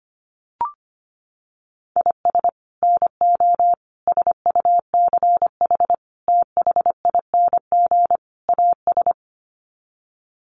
Morse code, heard as a keyed tone, about 25 wpm, 700 Hz. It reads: IS NO SUCH THING AS
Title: Sets of 5 Words Encore 25wpm